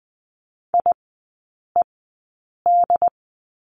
Morse code, heard as a keyed tone, about 20 wpm, spaced as for 8 wpm, 700 Hz. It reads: IED
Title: 087 Lesson 15e - Number 5 - Three Characters 20wpm